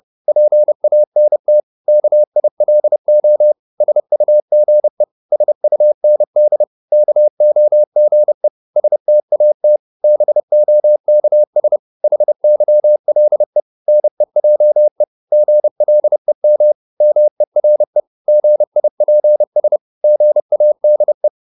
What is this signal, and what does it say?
Morse code, heard as a keyed tone, about 30 wpm, 600 Hz. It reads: PANT KILO SUGE SUND KOGE STAT BOKS HYLE NEJE GLEM MERE GIPS GADE
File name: Korte ord DK 30wpm